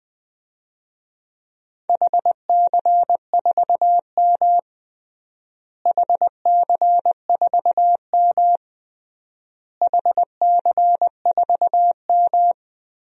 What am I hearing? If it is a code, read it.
Morse code, heard as a keyed tone, about 20 wpm, 700 Hz. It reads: HC4M HC4M HC4M
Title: DXCC-100 Call Signs - 4 Characters - Repeated 3x - 20wpm